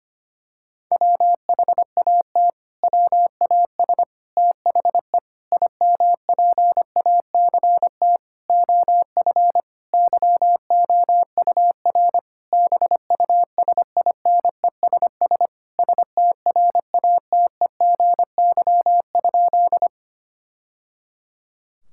Morse code, heard as a keyed tone, about 25 wpm, 700 Hz. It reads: WHAT WAS THE IMPACT OF YOUR BUSINESS STRATEGY?
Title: Sentences from Top 1000 Words - Review Part 6 - 25wpm